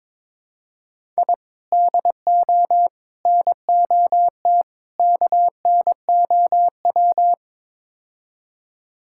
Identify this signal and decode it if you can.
Morse code, heard as a keyed tone, about 22 wpm, 700 Hz. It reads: I DO NOT KNOW
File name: practice / 22wpm / Build A Phrase - 5-Word Phrases - 22wpm